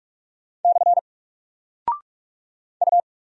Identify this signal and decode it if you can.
Morse code, heard as a keyed tone, about 44 wpm, 700 Hz. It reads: / U
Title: Single Character (Letter:Number:Punctuation:Prosign) 45wpm